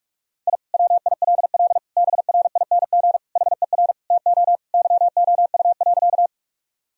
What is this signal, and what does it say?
Morse code, heard as a keyed tone, about 46 wpm, 700 Hz. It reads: I WILL BRING HER TO YOU.
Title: Sentences from Top 1000 Words - Review Part 3 - 45wpm